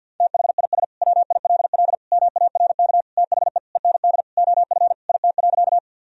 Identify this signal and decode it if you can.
Morse code, heard as a keyed tone, about 53 wpm, 700 Hz. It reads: THIS WILL MARK THE END OF IT.
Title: Sentences from Top 1000 Words - Review Part 1 - 50wpm